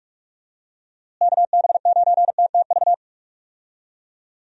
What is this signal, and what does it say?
Morse code, heard as a keyed tone, about 42 wpm, 700 Hz. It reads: KB9TTV
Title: USA Call Signs - 6 Characters - Repeated 3x - 45wpm